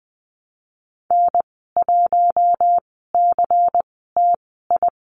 Morse code, heard as a keyed tone, about 19 wpm, spaced as for 14 wpm, 700 Hz. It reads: N1CTI
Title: 154 Lesson 25g - Letter p - Call Signs 20wpm